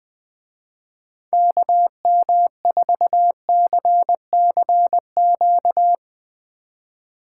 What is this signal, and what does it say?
Morse code, heard as a keyed tone, about 22 wpm, 700 Hz. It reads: KM4CCQ
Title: Speed Racing - USA 2x3 Call Signs 20wpm